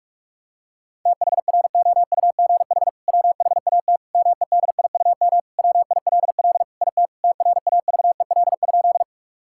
Morse code, heard as a keyed tone, about 45 wpm, 700 Hz. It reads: THROUGH WHAT MEDIUM WILL IT TRAVEL?
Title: Sentences from Top 1000 Words - Review Part 6 - 45wpm